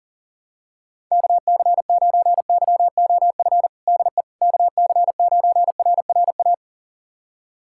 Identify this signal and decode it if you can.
Morse code, heard as a keyed tone, about 40 wpm, 700 Hz. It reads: KC9YOF DE KC9RRA
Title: USA Callsign DE Callsign 40wpm